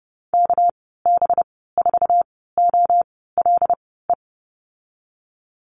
Morse code, heard as a keyed tone, about 30 wpm, spaced as for 17 wpm, 700 Hz. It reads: KB4OLE
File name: Lesson 37g - Letter x - Call Signs 30wpm